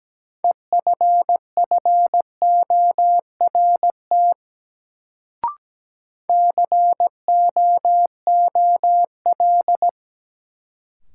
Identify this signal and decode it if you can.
Morse code, heard as a keyed tone, about 17 wpm, 700 Hz. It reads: EFFORT COOL
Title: Sentences from Top 1200 Words - New Word Drill Alt - 17wpm